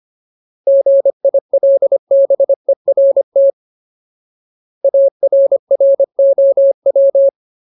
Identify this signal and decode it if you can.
Morse code, heard as a keyed tone, about 25 wpm, 550 Hz. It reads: GILBERT ARROW
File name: Ultimate Word List / 25WPM Ultimate